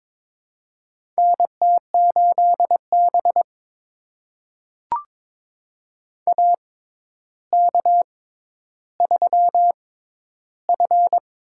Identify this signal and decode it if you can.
Morse code, heard as a keyed tone, about 22 wpm, spaced as for 11 wpm, 700 Hz. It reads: NT8B AK3F
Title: USA Call Signs - 6x Character Spacing - 22wpm